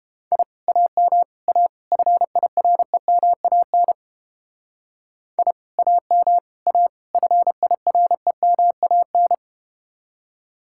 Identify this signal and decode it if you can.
Morse code, heard as a keyed tone, about 31 wpm, 700 Hz. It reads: I AM A FIREMAN I AM A FIREMAN